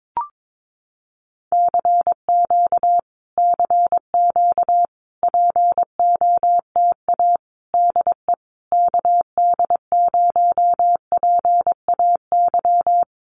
Morse code, heard as a keyed tone, about 22 wpm, 700 Hz. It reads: CQ CQ POTA DE KD0PAY
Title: Common QSO Phrases Encore 22wpm